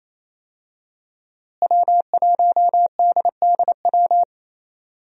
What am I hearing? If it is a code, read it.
Morse code, heard as a keyed tone, about 28 wpm, 700 Hz. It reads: W1DDW
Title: USA Call Signs - 5 Characters - Repeated 2x - 28wpm